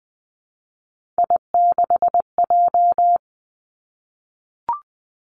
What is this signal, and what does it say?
Morse code, heard as a keyed tone, about 20 wpm, 700 Hz. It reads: I6J
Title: Three Challenging Letter-Number Combinations 20wpm